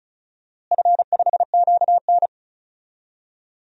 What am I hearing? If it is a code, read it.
Morse code, heard as a keyed tone, about 36 wpm, 690 Hz. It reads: F5QN
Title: DXCC-100 Call Signs - 4 Characters - Repeated 3x - 35wpm